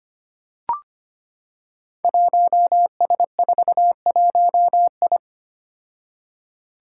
Morse code, heard as a keyed tone, about 25 wpm, 700 Hz. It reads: 1S41I
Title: Five Challenging Letter-Number Combinations 25wpm